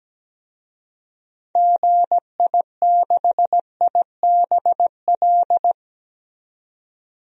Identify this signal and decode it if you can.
Morse code, heard as a keyed tone, about 17 wpm, 700 Hz. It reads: GI6IBL
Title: DXCC-100 Call Signs - 6 Characters - 17wpm